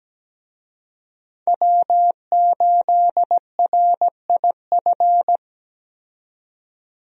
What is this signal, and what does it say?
Morse code, heard as a keyed tone, about 17 wpm, 700 Hz. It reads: W8RIF
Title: USA Call Signs - 5 Characters - 17wpm